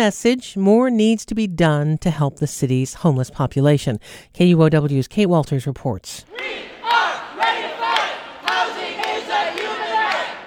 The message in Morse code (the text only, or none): none